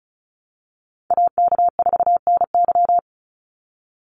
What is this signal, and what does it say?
Morse code, heard as a keyed tone, about 36 wpm, 700 Hz. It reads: AK4NY
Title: Call Signs - 35wpm